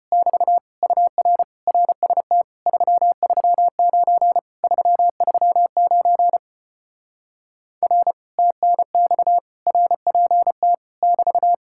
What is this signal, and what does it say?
Morse code, heard as a keyed tone, about 33 wpm, 700 Hz. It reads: <BT> UR RST 339 339 R TNX RPT <BT>